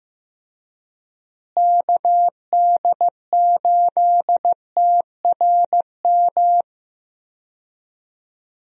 Morse code, heard as a keyed tone, about 15 wpm, 700 Hz. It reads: KD8TRM
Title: USA Call Signs - 6 Characters - Repeated 2x - 15wpm